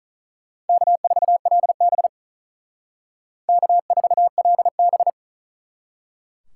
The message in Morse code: K4LB K4LB

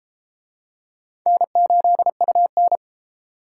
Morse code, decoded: N8UN

